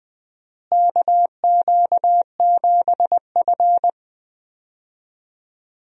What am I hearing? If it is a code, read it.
Morse code, heard as a keyed tone, about 20 wpm, 700 Hz. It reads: KQ7F
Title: USA Call Signs - 4 Characters - Repeated 2x - 20wpm